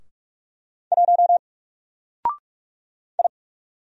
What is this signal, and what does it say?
Morse code, heard as a keyed tone, about 44 wpm, 700 Hz. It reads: 1 I
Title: Single Character (Letter:Number:Punctuation:Prosign) Alt 45wpm